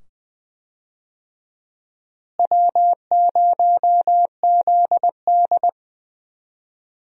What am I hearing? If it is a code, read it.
Morse code, heard as a keyed tone, about 20 wpm, 700 Hz. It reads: W0ZD
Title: USA Call Signs - 4 Characters - Repeated 2x - 20wpm